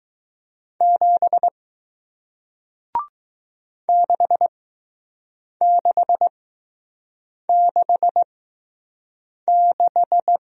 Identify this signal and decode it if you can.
Morse code, heard as a keyed tone, about 20 wpm, spaced as for 6 wpm, 700 Hz. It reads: 7 6666